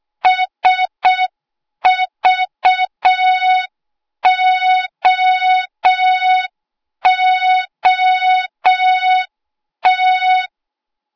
Morse code, heard as a keyed tone, about 6 wpm, 3700 Hz. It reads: SVOOT
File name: Morze-HA2MN / hang / 08KXC HA2MN